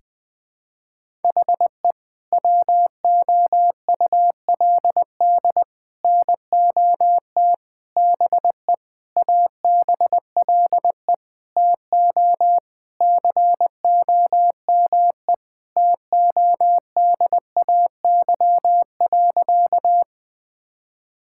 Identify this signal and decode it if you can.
Morse code, heard as a keyed tone, about 20 wpm, 700 Hz. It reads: HE WOULD NOT BE ABLE TO COME TODAY.
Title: Sentences with 5-Word Phrases - 20wpm